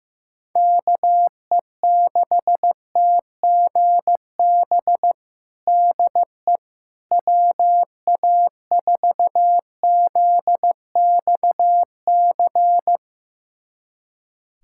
Morse code, heard as a keyed tone, about 15 wpm, 700 Hz. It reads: KE6TGB DE WA4ZXC
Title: USA Callsign DE Callsign 15wpm